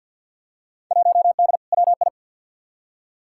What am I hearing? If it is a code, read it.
Morse code, heard as a keyed tone, about 53 wpm, 700 Hz. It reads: 1D WI
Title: Field Day Exchanges - 50wpm